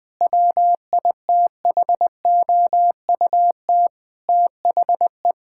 Morse code, heard as a keyed tone, about 20 wpm, 700 Hz. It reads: WITHOUT THE
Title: Sentences with 7-Word Phrases - 20wpm